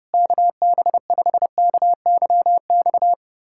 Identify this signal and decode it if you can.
Morse code, heard as a keyed tone, about 32 wpm, 700 Hz. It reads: KB5KYX
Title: K1USN SST QSO Practice - 30wpm